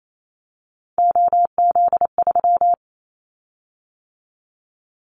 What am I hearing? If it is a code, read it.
Morse code, heard as a keyed tone, about 28 wpm, 700 Hz. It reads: OZ3
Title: Callsign Prefixes for Top 100 DXCC - 28wpm